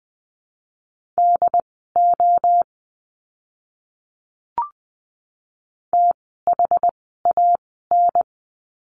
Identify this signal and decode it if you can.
Morse code, heard as a keyed tone, about 20 wpm, spaced as for 14 wpm, 700 Hz. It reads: DO THAN
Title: 185 Lesson 30f - Letter b - Words 20wpm